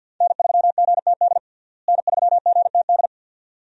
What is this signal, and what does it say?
Morse code, heard as a keyed tone, about 48 wpm, 700 Hz. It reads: N3GTD N3GTD
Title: USA Call Signs - 5 Characters - Repeated 2x - 50wpm